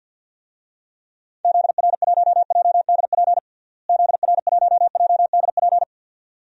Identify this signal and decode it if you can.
Morse code, heard as a keyed tone, about 52 wpm, 700 Hz. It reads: ZR1JDP ZR1JDP
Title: DXCC-100 Call Signs - 6 Characters - Repeated 2x - 50wpm